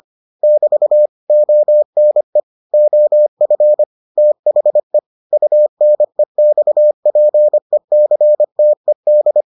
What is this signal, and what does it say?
Morse code, heard as a keyed tone, about 25 wpm, 600 Hz. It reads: <BT> ONE OF THE UNEXPECTED